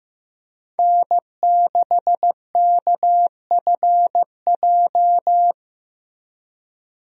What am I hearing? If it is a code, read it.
Morse code, heard as a keyed tone, about 15 wpm, 700 Hz. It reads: N6KFJ
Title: USA Call Signs - 5 Characters - 15wpm